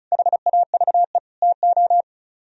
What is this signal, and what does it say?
Morse code, heard as a keyed tone, about 43 wpm, 690 Hz. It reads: HAVE TO
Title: Speed-Racing - Sets of 3 Words 35wpm